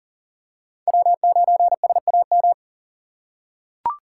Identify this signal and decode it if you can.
Morse code, heard as a keyed tone, about 40 wpm, 700 Hz. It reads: W9SAM
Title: USA Call Signs - 5 Characters - 4x Character Spacing - 40wpm